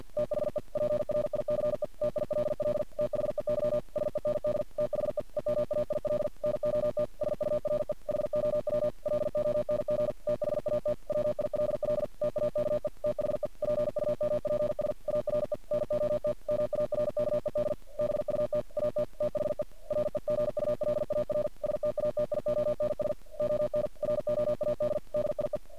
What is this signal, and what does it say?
Morse code, heard as a keyed tone, about 50 wpm, 600 Hz. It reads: THE PRICE TILL THEY SEND THE EMAIL NOT SURE HOW LONG THAT WILL TAKE THE PUMPS ARE NOT MARKED BUT AT THE REGULAR STATIONS ON ROAD DIE
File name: archived / CRSnet